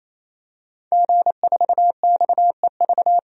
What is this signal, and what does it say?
Morse code, heard as a keyed tone, about 28 wpm, 700 Hz. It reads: G4XEV
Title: DXCC-100 Call Signs - 5 Characters - 28wpm